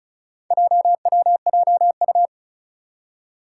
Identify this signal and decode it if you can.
Morse code, heard as a keyed tone, about 36 wpm, 700 Hz. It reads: JWJU
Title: Four Challenging Letter-Number Combinations 35wpm